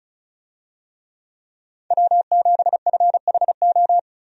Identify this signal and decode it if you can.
Morse code, heard as a keyed tone, about 34 wpm, 700 Hz. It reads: W7FHO
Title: USA Call Signs - 5 Characters - 35wpm